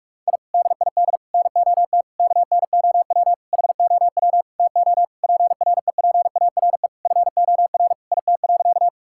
Morse code, heard as a keyed tone, about 41 wpm, 700 Hz. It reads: I DID NOT KNOW HOW TO PREPARE FOR IT.
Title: Sentences from Top 1000 Words - Review Part 6 - 45wpm